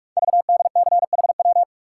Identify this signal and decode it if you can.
Morse code, heard as a keyed tone, about 44 wpm, 700 Hz. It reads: VDCHW
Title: Code Groups - 5 Letters - 45wpm